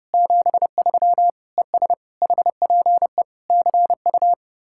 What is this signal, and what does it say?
Morse code, heard as a keyed tone, about 36 wpm, 700 Hz. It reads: 73 ES HPE CU
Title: Speed-Racing Common QSO Phrases - 30wpm